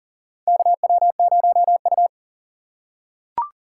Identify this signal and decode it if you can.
Morse code, heard as a keyed tone, about 40 wpm, 700 Hz. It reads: KW0U
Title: USA Call Signs - 4 Characters - 2x Character Spacing - 40wpm